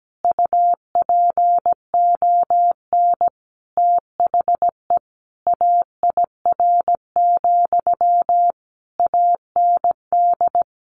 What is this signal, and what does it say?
Morse code, heard as a keyed tone, about 17 wpm, 700 Hz. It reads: UPON THE AIR, AND
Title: A Sound Of Thunder 17wpm